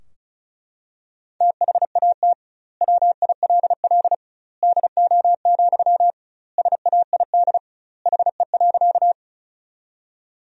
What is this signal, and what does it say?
Morse code, heard as a keyed tone, about 36 wpm, 700 Hz. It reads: THAT WILL DO, SAID HE.
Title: 2x Word-Spacing - Sentences from Top 100 Words - 35wpm